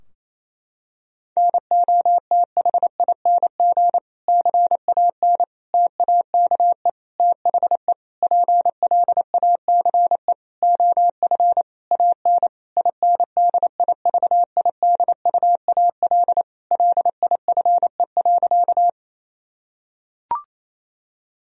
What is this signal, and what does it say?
Morse code, heard as a keyed tone, about 28 wpm, 700 Hz. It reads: NOTHING CAN TAKE THE PLACE OF AN INDIVIDUAL LIFE.
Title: Sentences from Top 1000 Words - Review Part 4 - 28wpm